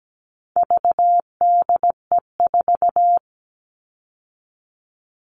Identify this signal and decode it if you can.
Morse code, heard as a keyed tone, about 17 wpm, 700 Hz. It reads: VDE4
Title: All Cut Numbers - 4-Digit 17wpm